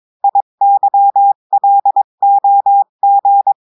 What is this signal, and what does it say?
Morse code, heard as a keyed tone, about 22 wpm, 800 Hz. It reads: IYLOG
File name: cours-du-mercredi / audios / cours-853-du-2-mai-2024-etxte chiffre et texte clair-a-20-wpm